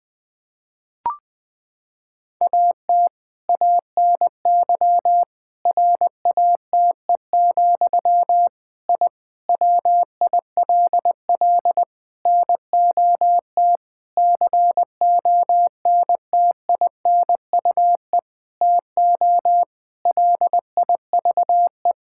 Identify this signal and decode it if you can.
Morse code, heard as a keyed tone, about 20 wpm, 700 Hz. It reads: AT ANY RATE, I WILL NOT CONTINUE TO LIVE
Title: Sentences from Top 1000 Words - Review Part 6 - 20wpm